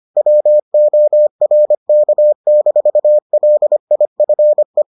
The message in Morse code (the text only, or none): WORK-LIFE